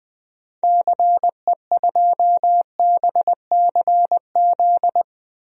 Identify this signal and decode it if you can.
Morse code, heard as a keyed tone, about 20 wpm, 700 Hz. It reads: CE2BCZ
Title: DXCC-100 Call Signs - 6 Characters - 20wpm